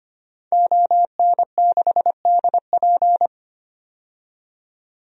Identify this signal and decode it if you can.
Morse code, heard as a keyed tone, about 25 wpm, 700 Hz. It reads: ON6DP
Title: DXCC-100 Call Signs - 5 Characters - 25wpm